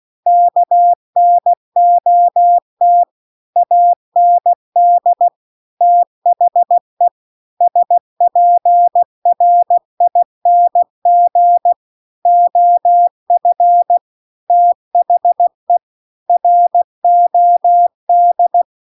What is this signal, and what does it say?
Morse code, heard as a keyed tone, about 16 wpm, 700 Hz. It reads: KNOT AND THE SPRING OF THE ROD